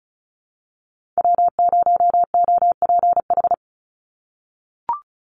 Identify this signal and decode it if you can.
Morse code, heard as a keyed tone, about 34 wpm, 700 Hz. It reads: W0OPH
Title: USA Call Signs 35wpm